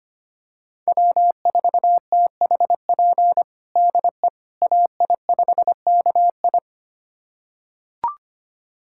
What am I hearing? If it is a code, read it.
Morse code, heard as a keyed tone, about 25 wpm, 700 Hz. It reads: W4THP DE AI5KI
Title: USA Callsign DE Callsign - 5 Characters - 25wpm